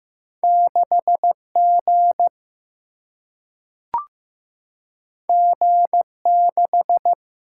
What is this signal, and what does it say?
Morse code, heard as a keyed tone, about 15 wpm, 700 Hz. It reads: 6 G G 6
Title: Single-Letter Single-Number 15wpm